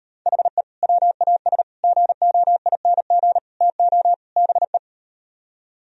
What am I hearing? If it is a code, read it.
Morse code, heard as a keyed tone, about 39 wpm, 700 Hz. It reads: HE WAS GOING TO BE